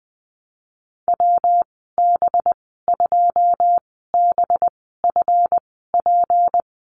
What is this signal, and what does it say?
Morse code, heard as a keyed tone, about 20 wpm, spaced as for 14 wpm, 700 Hz. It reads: WB2BFP
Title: 238 Lesson 38g - Letter q - Call Signs 20wpm